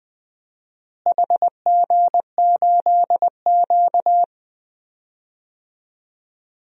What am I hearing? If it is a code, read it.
Morse code, heard as a keyed tone, about 20 wpm, 700 Hz. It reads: HG8Q
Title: DXCC-100 Call Signs - 4 Characters - Repeated 2x - 20wpm